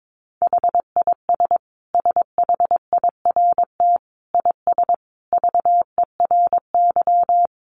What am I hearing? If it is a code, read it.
Morse code, heard as a keyed tone, about 22 wpm, 700 Hz. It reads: HIS SHIRT IS VERY
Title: Additional Easy Phrases 22wpm